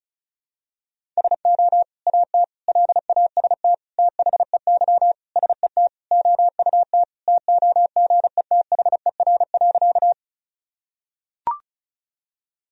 Morse code, heard as a keyed tone, about 36 wpm, 700 Hz. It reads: SO AT LAST THEY SET OUT TOGETHER.
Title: Sentences from Top 1000 Words - Review Part 1 - 35wpm